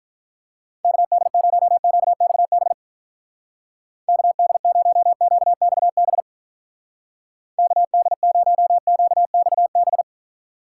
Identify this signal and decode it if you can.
Morse code, heard as a keyed tone, about 46 wpm, 700 Hz. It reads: KD0QXB KD0QXB KD0QXB